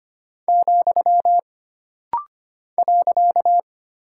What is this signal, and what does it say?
Morse code, heard as a keyed tone, about 25 wpm, 700 Hz. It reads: , .
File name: Single Character (Letter:Number:Punctuation:Prosign) - Rapid-Fire - 25wpm